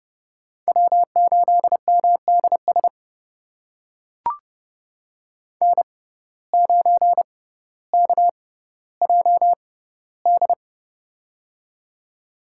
Morse code, heard as a keyed tone, about 30 wpm, spaced as for 15 wpm, 700 Hz. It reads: W8MDS N9KJD
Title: USA Call Signs - 5 Characters - 6x Character Spacing - 30wpm